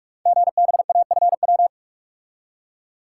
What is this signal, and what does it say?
Morse code, heard as a keyed tone, about 44 wpm, 700 Hz. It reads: GBAFW
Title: Code Groups - 5 Letters - 45wpm